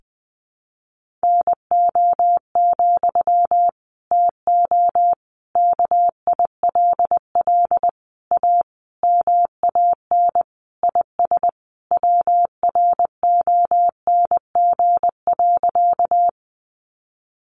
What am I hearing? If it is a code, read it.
Morse code, heard as a keyed tone, about 20 wpm, 700 Hz. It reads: NO, TO KILL A MAN IS WRONG.
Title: Sentences from Top 1400 Words 20wpm